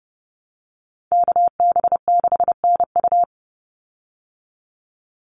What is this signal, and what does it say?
Morse code, heard as a keyed tone, about 30 wpm, 700 Hz. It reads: KB6NU
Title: Call Signs Repeated 2x - 30wpm